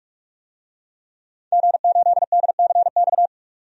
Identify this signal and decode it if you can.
Morse code, heard as a keyed tone, about 46 wpm, 700 Hz. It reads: G8DCX
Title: DXCC-100 Call Signs - 5 Characters - 45wpm